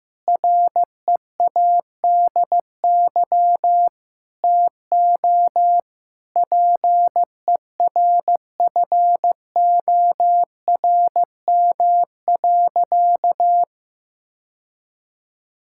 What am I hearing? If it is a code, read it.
Morse code, heard as a keyed tone, about 16 wpm, 700 Hz. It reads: READY TO PERFORM.
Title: Sentences from Top 1000 Words - Review Part 5 - 15wpm